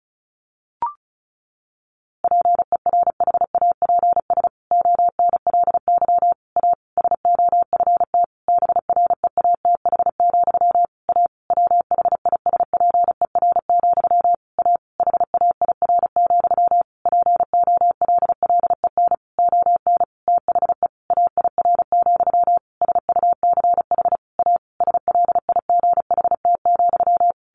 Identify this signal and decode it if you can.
Morse code, heard as a keyed tone, about 34 wpm, 700 Hz. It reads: PERHAPS ONLY A SOFT BREATH, A WHISPER, A HAIR, POLLEN ON THE AIR, SUCH A SLIGHT,
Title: A Sound Of Thunder 35wpm